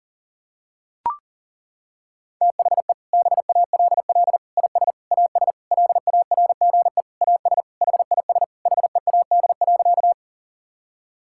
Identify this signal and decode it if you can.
Morse code, heard as a keyed tone, about 37 wpm, 700 Hz. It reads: THE BALL IS AS LARGE AS HIS HEAD.
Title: Sentences from Top 1000 Words - Review Part 6 - 40wpm